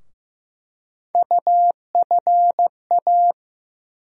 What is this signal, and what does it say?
Morse code, heard as a keyed tone, about 15 wpm, 700 Hz. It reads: UFA
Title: USA Call Sign Postfixes - 3 Letters 15wpm